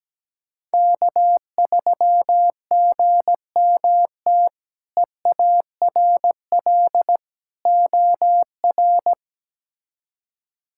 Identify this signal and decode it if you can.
Morse code, heard as a keyed tone, about 17 wpm, 700 Hz. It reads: K3GMT EARL OR
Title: K1USN SST QSO Practice - 17wpm